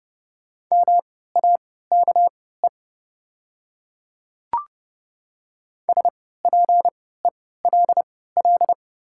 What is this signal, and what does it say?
Morse code, heard as a keyed tone, about 30 wpm, spaced as for 17 wpm, 700 Hz. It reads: MAKE SPELL
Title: 3x Character Spacing - Top 300 Words - 30wpm